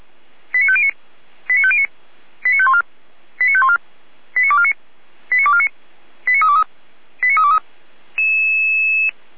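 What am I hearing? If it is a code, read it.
Five-tone selective call (ZVEI), heard as quick runs of five tones, tones of about 70 ms. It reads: 89509 89509 89514 89514 89249 89249 89323 89323
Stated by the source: ZVEI five-tone call received by SDR.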